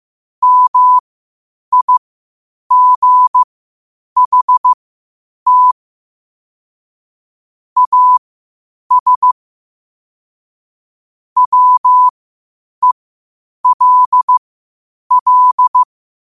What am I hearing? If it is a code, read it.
Morse code, heard as a keyed tone, about 15 wpm, spaced as for 8 wpm, 1000 Hz. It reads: MIGHT AS WELL
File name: qotdfiles / 8wpm